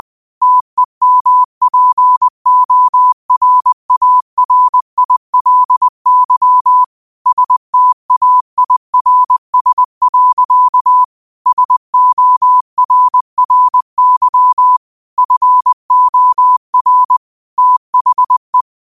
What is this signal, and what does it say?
Morse code, heard as a keyed tone, about 20 wpm, 1000 Hz. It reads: TEMPORARILY STAIRS. SORRY FOR THE
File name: qotdfiles / 20wpm